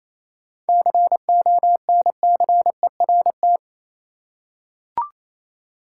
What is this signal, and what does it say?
Morse code, heard as a keyed tone, about 28 wpm, 700 Hz. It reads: CONCERT
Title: Seven Letter Words - 28wpm